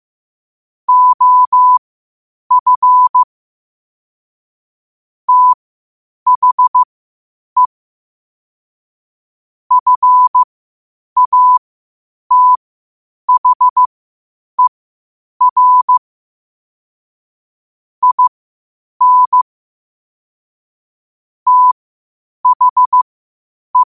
Morse code, heard as a keyed tone, about 15 wpm, spaced as for 8 wpm, 1000 Hz. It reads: OF THE FATHER IN THE
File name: qotdfiles / 8wpm